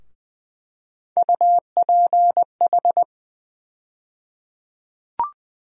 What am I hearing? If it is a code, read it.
Morse code, heard as a keyed tone, about 20 wpm, 700 Hz. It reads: UPH
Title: USA Call Sign Postfixes - 3 Letters 20wpm